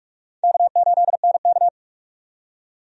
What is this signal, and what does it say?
Morse code, heard as a keyed tone, about 46 wpm, 700 Hz. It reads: K8NK
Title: USA Call Signs - 4 Characters - 45wpm